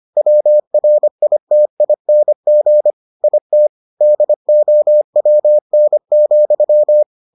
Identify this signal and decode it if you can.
Morse code, heard as a keyed tone, about 25 wpm, 600 Hz. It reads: WRITING IT DOWN,